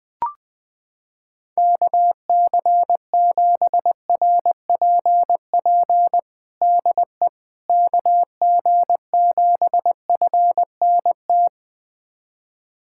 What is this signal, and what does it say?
Morse code, heard as a keyed tone, about 20 wpm, 700 Hz. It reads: KC7RPP DE KG7FNT
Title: USA Callsign DE Callsign - 6 Characters - 20wpm